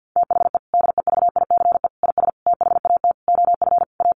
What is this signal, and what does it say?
Morse code, heard as a keyed tone, about 48 wpm, 700 Hz. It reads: THE DEVICE IS THAT OF A
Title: Sentences from Top 1500 Words 50wpm